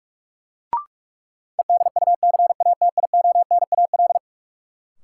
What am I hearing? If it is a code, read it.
Morse code, heard as a keyed tone, about 44 wpm, 700 Hz. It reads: EDUCATIONAL
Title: Sentences from Top 1200 Words - New Word Drill Alt - 45wpm